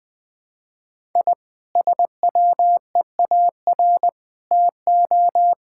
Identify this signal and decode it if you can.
Morse code, heard as a keyed tone, about 21 wpm, 700 Hz. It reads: I SWEAR TO
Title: Sentences with 9-Word Phrases - Spoken Beginning - 20wpm